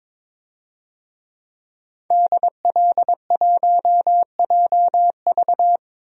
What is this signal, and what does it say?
Morse code, heard as a keyed tone, about 22 wpm, 700 Hz. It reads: DL1JV
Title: DXCC-100 Call Signs - 5 Characters - 22wpm